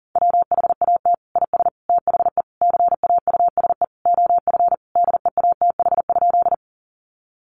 Morse code, heard as a keyed tone, about 40 wpm, 700 Hz. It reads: WHAT IS THE CAUSE OF DEATH?
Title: Sentences from Top 1100 Words 40wpm